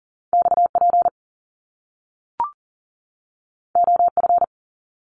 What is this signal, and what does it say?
Morse code, heard as a keyed tone, about 42 wpm, 700 Hz. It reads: XP OF
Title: Top Two Letter Combinations in Words 40wpm